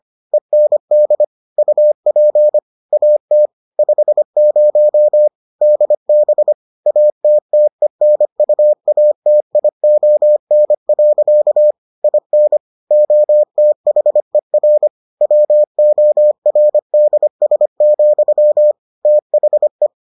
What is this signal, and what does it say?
Morse code, heard as a keyed tone, about 25 wpm, 600 Hz. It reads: END UP AT 50 DB ATTENUATION. IN OTHER WORDS, THE